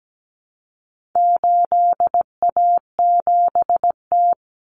Morse code, heard as a keyed tone, about 17 wpm, 700 Hz. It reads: 8A7T
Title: Common Cut Numbers - 4-Digit 17wpm